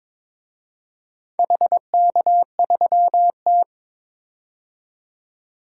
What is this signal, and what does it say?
Morse code, heard as a keyed tone, about 22 wpm, 700 Hz. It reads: HK3T
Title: DXCC-100 Call Signs - 4 Characters - 22wpm